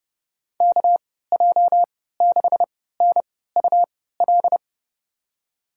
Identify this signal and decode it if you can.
Morse code, heard as a keyed tone, about 31 wpm, spaced as for 17 wpm, 700 Hz. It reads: KJ6NUL
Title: USA Call Signs - 6 Characters - 3x Character Spacing - 30wpm